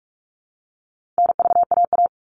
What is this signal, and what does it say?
Morse code, heard as a keyed tone, about 44 wpm, 700 Hz. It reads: NVAA
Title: All Cut Numbers - 4-Digit 45wpm